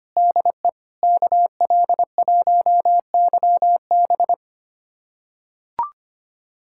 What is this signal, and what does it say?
Morse code, heard as a keyed tone, about 25 wpm, 700 Hz. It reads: DE KL1YB
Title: USA Callsign DE Callsign - 5 Characters - 25wpm